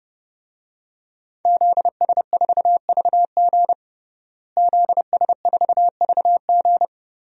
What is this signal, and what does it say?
Morse code, heard as a keyed tone, about 32 wpm, 700 Hz. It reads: ZS4VG ZS4VG
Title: DXCC-100 Call Signs - 5 Characters - Repeated 3x - 30wpm